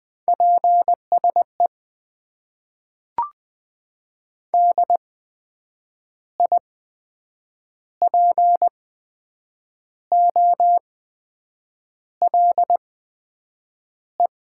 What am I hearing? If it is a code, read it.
Morse code, heard as a keyed tone, about 20 wpm, spaced as for 7 wpm, 700 Hz. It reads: PSE DIPOLE
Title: Top 90 QSO Elements - 8x Character Spacing - 20wpm